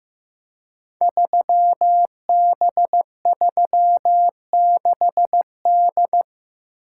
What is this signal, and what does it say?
Morse code, heard as a keyed tone, about 15 wpm, 700 Hz. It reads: 3B36D
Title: Code Groups - 5 Letter:Numbers - 15wpm